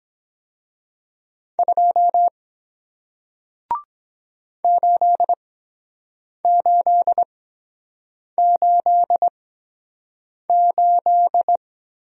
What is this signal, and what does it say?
Morse code, heard as a keyed tone, about 23 wpm, spaced as for 7 wpm, 700 Hz. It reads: 2 8888